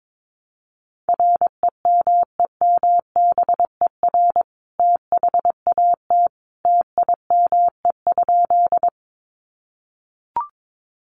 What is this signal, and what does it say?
Morse code, heard as a keyed tone, about 22 wpm, 700 Hz. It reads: REMEMBER THAT TIME?
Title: Sentences from Top 400 Words 22wpm